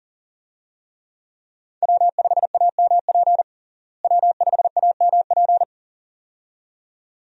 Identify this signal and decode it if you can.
Morse code, heard as a keyed tone, about 40 wpm, 700 Hz. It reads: W5AMP W5AMP
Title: USA Call Signs - 5 Characters - Repeated 2x - 40wpm